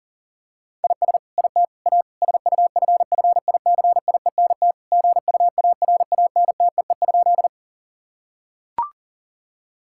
Morse code, heard as a keyed tone, about 40 wpm, 700 Hz. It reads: IS IT A SUFFICIENT GUARANTEE?
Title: Sentences from Top 1600 Words 40wpm